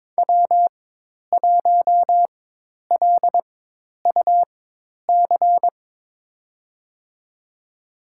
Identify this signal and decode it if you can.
Morse code, heard as a keyed tone, about 22 wpm, spaced as for 10 wpm, 700 Hz. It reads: W1LUC
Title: USA Call Signs - 4x Character Spacing - 22wpm